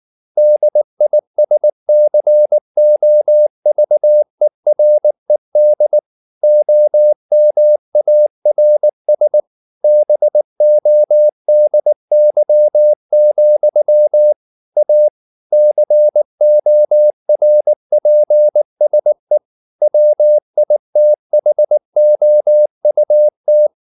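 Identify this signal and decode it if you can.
Morse code, heard as a keyed tone, about 19 wpm, 600 Hz. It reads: DISCOVERED OMARS BODY, A CORPSE WITHOUT
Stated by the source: Morse code at 19 wpm from CW med Gnister, reading Ken Follett's novel Never in English.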